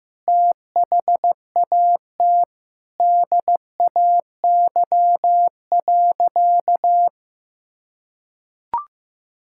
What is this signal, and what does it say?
Morse code, heard as a keyed tone, about 15 wpm, 700 Hz. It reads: THAT DAY.
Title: Sentences from Top 1000 Words - Review Part 5 - 15wpm